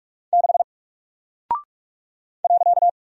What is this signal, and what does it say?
Morse code, heard as a keyed tone, about 47 wpm, 700 Hz. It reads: 6 .
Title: Single Character (Letter:Number:Punctuation:Prosign) Alt 45wpm